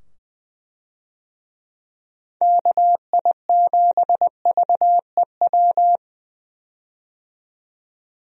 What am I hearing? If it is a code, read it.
Morse code, heard as a keyed tone, about 20 wpm, 700 Hz. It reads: KI7VEW
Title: USA Call Signs - 6 Characters - 20wpm